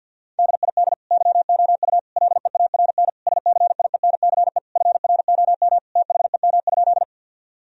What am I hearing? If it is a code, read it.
Morse code, heard as a keyed tone, about 52 wpm, 700 Hz. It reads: DID YOU LEARN SCIENCE FROM THEM?
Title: Sentences from Top 1000 Words - Review Part 2 - 50wpm